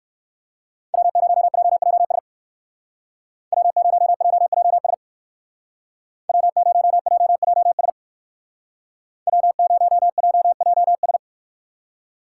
Morse code, heard as a keyed tone, about 64 wpm, 700 Hz. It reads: W0JJS W0JJS W0JJS W0JJS